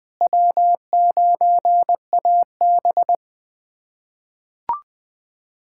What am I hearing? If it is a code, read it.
Morse code, heard as a keyed tone, about 20 wpm, 700 Hz. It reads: W9AB
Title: USA Callsign DE Callsign - 4 Characters - 20wpm